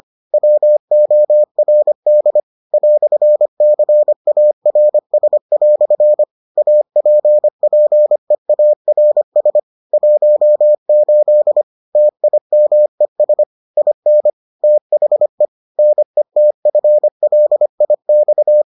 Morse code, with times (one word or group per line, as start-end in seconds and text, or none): 0.3-2.4 WORD
2.7-6.2 "CARS"
6.6-9.6 APPEARS
9.9-11.6 18
12.0-13.4 TIMES
13.8-14.3 IN
14.6-15.4 THE
15.8-18.6 NETFLIX